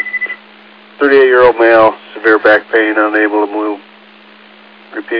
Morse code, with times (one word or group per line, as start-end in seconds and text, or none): none